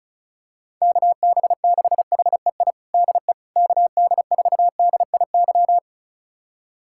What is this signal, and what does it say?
Morse code, heard as a keyed tone, about 36 wpm, 700 Hz. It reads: KB6HEI DE KD4DIY
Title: USA Callsign DE Callsign - 6 Characters - 35wpm